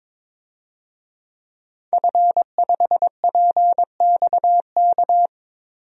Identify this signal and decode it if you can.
Morse code, heard as a keyed tone, about 22 wpm, 700 Hz. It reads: F5PXK
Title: DXCC-100 Call Signs - 5 Characters - 22wpm